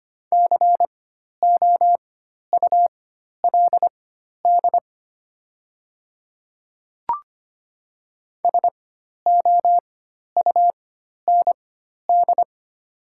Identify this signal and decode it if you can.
Morse code, heard as a keyed tone, about 25 wpm, spaced as for 12 wpm, 700 Hz. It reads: COULD SOUND
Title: 4x Character Spacing - Top 100 Words - 25wpm